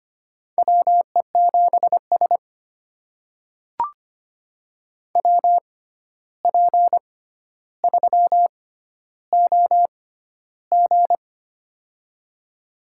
Morse code, heard as a keyed tone, about 25 wpm, spaced as for 12 wpm, 700 Hz. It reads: WE7S WP3OG
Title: USA Call Signs - 6x Character Spacing - 25wpm